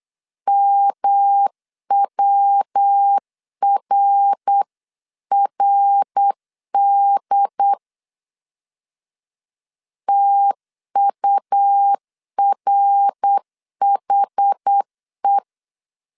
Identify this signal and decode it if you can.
Morse code, heard as a keyed tone, about 8 wpm, 800 Hz. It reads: MWRRD TURHE